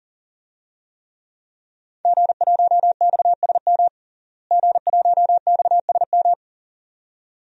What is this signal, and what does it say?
Morse code, heard as a keyed tone, about 40 wpm, 700 Hz. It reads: G1XSM G1XSM
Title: DXCC-100 Call Signs - 5 Characters - Repeated 2x - 40wpm